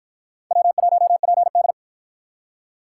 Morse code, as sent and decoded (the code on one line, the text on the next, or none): .-- .---- .--. -..
W1PD